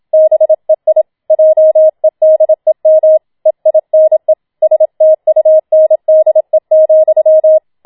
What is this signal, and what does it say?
Morse code, heard as a keyed tone, about 27 wpm, 630 Hz. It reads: BEI JEDEM EINE STUNDE,
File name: Schneewittchen Teil2 27wpm